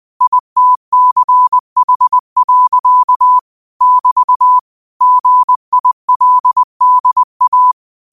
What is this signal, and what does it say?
Morse code, heard as a keyed tone, about 20 wpm, 1000 Hz. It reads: ITCH. <BT> GILDA
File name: qotdfiles / 20wpm